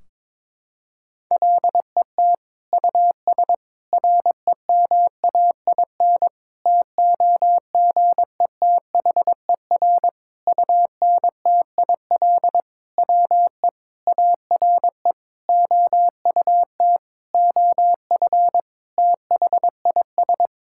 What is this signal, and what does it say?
Morse code, heard as a keyed tone, about 21 wpm, 700 Hz. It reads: LET US REMAIN TOGETHER UNTIL WE ARE OUT OF THIS
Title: Sentences from Top 1600 Words 22wpm